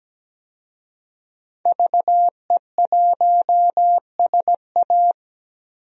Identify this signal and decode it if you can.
Morse code, heard as a keyed tone, about 17 wpm, 700 Hz. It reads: VE1SA
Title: DXCC-100 Call Signs - 5 Characters - 17wpm